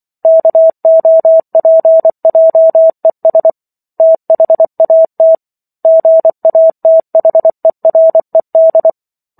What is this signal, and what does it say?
Morse code, heard as a keyed tone, about 24 wpm, 650 Hz. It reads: KOPJES THAT GATHERED